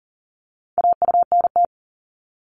Morse code, heard as a keyed tone, about 38 wpm, 700 Hz. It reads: AUNT
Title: Multi-letter Phonemes 40wpm